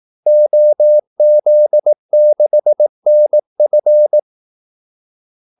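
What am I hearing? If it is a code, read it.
Morse code, heard as a keyed tone, about 18 wpm, 600 Hz. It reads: OZ6NF
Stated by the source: Morse call-sign practice at 18 wpm.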